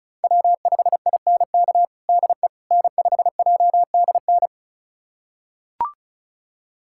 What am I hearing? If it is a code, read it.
Morse code, heard as a keyed tone, about 34 wpm, 700 Hz. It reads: W5INK DE N5JDN
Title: USA Callsign DE Callsign - 5 Characters - 35wpm